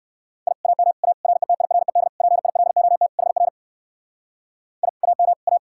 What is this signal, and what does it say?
Morse code, heard as a keyed tone, about 59 wpm, 700 Hz. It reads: I AM A RETIRED PILOT HR I AM A